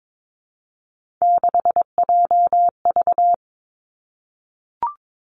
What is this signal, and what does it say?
Morse code, heard as a keyed tone, about 22 wpm, 700 Hz. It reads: 6JV
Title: Three Challenging Letter-Number Combinations 22wpm